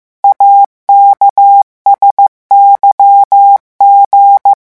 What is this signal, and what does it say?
Morse code, heard as a keyed tone, about 15 wpm, 800 Hz. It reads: AKSYG
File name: morse mp3 / revision11 lettres vitesse 14